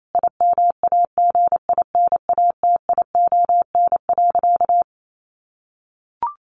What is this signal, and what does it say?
Morse code, heard as a keyed tone, about 28 wpm, 700 Hz. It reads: IMAGINATION.
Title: Sentences from Top 1300 Words 28wpm